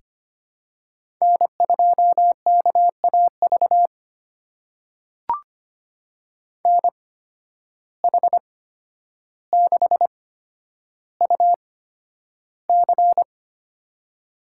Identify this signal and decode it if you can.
Morse code, heard as a keyed tone, about 25 wpm, spaced as for 11 wpm, 700 Hz. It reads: N2KAV NH6UC
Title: USA Call Signs - 5 Characters - 8x Character Spacing - 25wpm